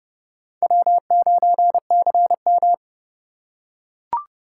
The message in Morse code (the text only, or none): W9CM